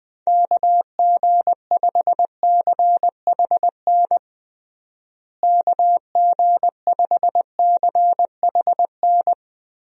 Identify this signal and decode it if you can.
Morse code, heard as a keyed tone, about 20 wpm, 700 Hz. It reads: KG5CHN KG5CHN
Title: USA Call Signs - 6 Characters - Repeated 3x - 20wpm